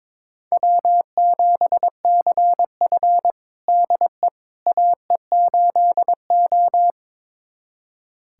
Morse code, heard as a keyed tone, about 22 wpm, 700 Hz. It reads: W7CF DE AE8O
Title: USA Callsign DE Callsign - 4 Characters - 22wpm